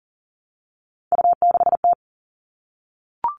Morse code, heard as a keyed tone, about 40 wpm, 700 Hz. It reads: U6T
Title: All Cut Numbers - 3-Digit 40wpm